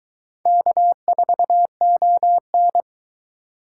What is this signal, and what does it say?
Morse code, heard as a keyed tone, about 22 wpm, 700 Hz. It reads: K4ON